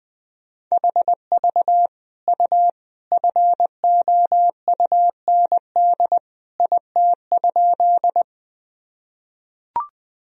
Morse code, heard as a keyed tone, about 20 wpm, 700 Hz. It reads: HV U FOUND IT?
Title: Sentences from Top 300 Words and Abbreviations 20wpm